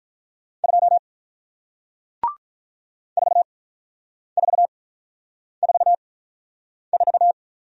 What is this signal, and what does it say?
Morse code, heard as a keyed tone, about 48 wpm, spaced as for 9 wpm, 700 Hz. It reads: 2 4444